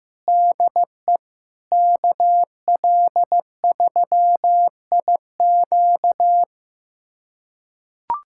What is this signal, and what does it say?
Morse code, heard as a keyed tone, about 16 wpm, 700 Hz. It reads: DE KL3IQ
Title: USA Callsign DE Callsign - 5 Characters - 15wpm